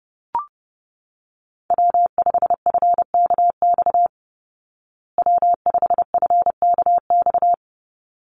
Morse code, heard as a keyed tone, about 30 wpm, 700 Hz. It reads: W5FKX W5FKX
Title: Call Signs Repeated 3x - 30wpm